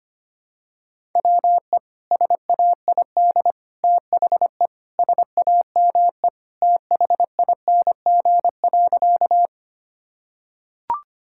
Morse code, heard as a keyed tone, about 25 wpm, 700 Hz. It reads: WE SAID THE SAME THING.
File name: practice / 25wpm / Sentences with 3-Word Phrases - 25wpm